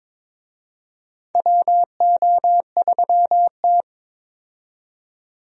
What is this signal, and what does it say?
Morse code, heard as a keyed tone, about 22 wpm, 700 Hz. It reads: WO3T
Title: USA Call Signs - 4 Characters - 22wpm